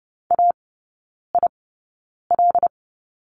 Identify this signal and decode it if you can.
Morse code, heard as a keyed tone, about 28 wpm, spaced as for 9 wpm, 690 Hz. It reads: AIL
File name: Lesson 16e - Letter c - Three Characters 30wpm